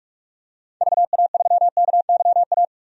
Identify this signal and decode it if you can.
Morse code, heard as a keyed tone, about 44 wpm, 700 Hz. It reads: VA3KYA
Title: DXCC-100 Call Signs - 6 Characters - 45wpm